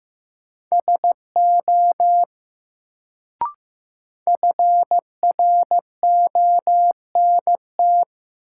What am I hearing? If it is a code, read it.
Morse code, heard as a keyed tone, about 15 wpm, 700 Hz. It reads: SO FRONT
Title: Sets of 2 Words - New Word Drill - 15wpm